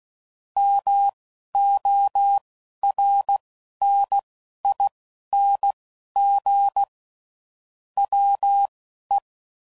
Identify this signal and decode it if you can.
Morse code, heard as a keyed tone, about 16 wpm, spaced as for 11 wpm, 800 Hz. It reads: MORNING WE
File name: CW Code Groups @10WPM / Weather Sets / Weather 3 17